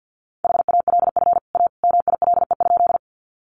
Morse code, heard as a keyed tone, about 50 wpm, 700 Hz. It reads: HALF A MILE?
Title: Sentences from Top 400 Words 50wpm